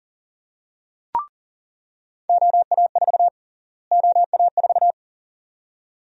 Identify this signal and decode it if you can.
Morse code, heard as a keyed tone, about 40 wpm, 700 Hz. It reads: OA4 OA4
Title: USA Call Sign Prefixes - Repeated 2x - 40wpm